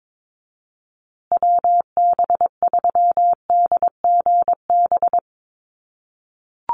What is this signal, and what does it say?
Morse code, heard as a keyed tone, about 22 wpm, 700 Hz. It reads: WB3DGB
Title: USA Call Signs Repeated 3x 22wpm